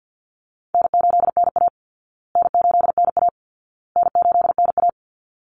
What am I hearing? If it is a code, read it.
Morse code, heard as a keyed tone, about 50 wpm, 700 Hz. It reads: N8NA N8NA N8NA
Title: Easy Call Signs Repeated 3x - 50wpm